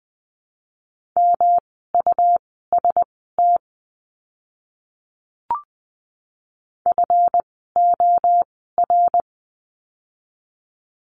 Morse code, heard as a keyed tone, about 20 wpm, spaced as for 14 wpm, 700 Hz. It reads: MUST FOR
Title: 237 Lesson 38f - Letter q - Words 20wpm